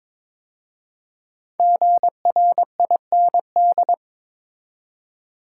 Morse code, heard as a keyed tone, about 22 wpm, 700 Hz. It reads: GRIND
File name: 5 Letter Words - First 3 Letters Given - 22wpm